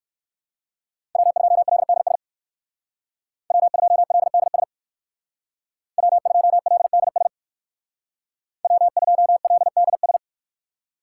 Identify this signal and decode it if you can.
Morse code, heard as a keyed tone, about 56 wpm, 700 Hz. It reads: W2LDS W2LDS W2LDS W2LDS
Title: Speed Racing - USA 1x3 Call Signs 45wpm